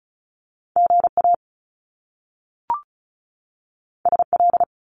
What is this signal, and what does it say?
Morse code, heard as a keyed tone, about 36 wpm, 700 Hz. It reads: GA SL
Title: Top Two Letter Combinations in Words 35wpm